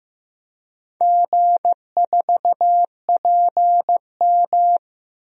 Morse code, heard as a keyed tone, about 15 wpm, 700 Hz. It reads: G4PM
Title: DXCC-100 Call Signs - 4 Characters - 15wpm